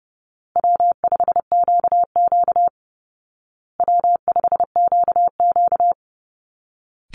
Morse code, heard as a keyed tone, about 30 wpm, 700 Hz. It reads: W5QQ W5QQ
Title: Easy Call Signs Repeated 3x - 30wpm